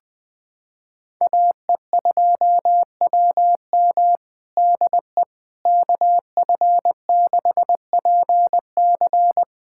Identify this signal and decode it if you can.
Morse code, heard as a keyed tone, about 20 wpm, 700 Hz. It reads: AE2WM DE KF6PC
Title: USA Callsign DE Callsign - 5 Characters - 20wpm